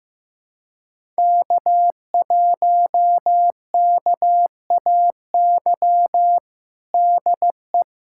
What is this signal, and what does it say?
Morse code, heard as a keyed tone, about 15 wpm, 700 Hz. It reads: K1KAY DE
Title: USA Callsign DE Callsign - 5 Characters - 15wpm